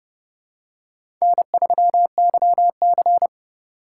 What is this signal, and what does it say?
Morse code, heard as a keyed tone, about 30 wpm, 690 Hz. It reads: N3YC
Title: USA Call Signs - 4 Characters - 2x Character Spacing - 30wpm